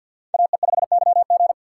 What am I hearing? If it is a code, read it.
Morse code, heard as a keyed tone, about 63 wpm, 700 Hz. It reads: AE5YG